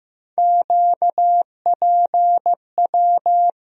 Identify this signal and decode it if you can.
Morse code, heard as a keyed tone, about 15 wpm, 700 Hz. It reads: QPW
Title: USA Call Sign Postfixes - 3 Letters 15wpm